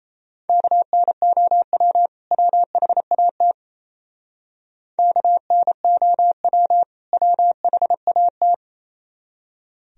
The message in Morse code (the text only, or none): KNOW WHAT KNOW WHAT